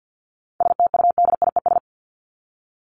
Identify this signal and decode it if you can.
Morse code, heard as a keyed tone, about 48 wpm, 700 Hz. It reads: STUDIES
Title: ES:S Suffix - Top 100 Words with Base Words 50wpm